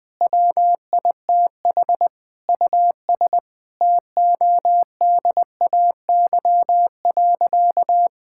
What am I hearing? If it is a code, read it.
Morse code, heard as a keyed tone, about 21 wpm, 700 Hz. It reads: WITH US TODAY.
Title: Sentences with 9-Word Phrases - 20wpm